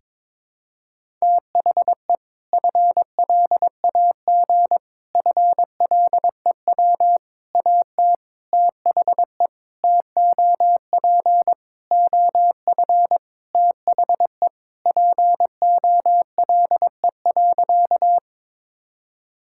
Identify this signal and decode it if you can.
Morse code, heard as a keyed tone, about 22 wpm, 700 Hz. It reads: THE FLAG FLEW AT THE TOP OF THE POLE.
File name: practice / 22wpm / Sentences with 4-Word Phrases - 22wpm